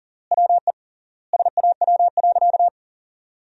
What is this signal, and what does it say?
Morse code, heard as a keyed tone, about 40 wpm, 700 Hz. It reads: WE SAW.
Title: 3x Word-Spacing - Sentences from Top 300 Words - 40wpm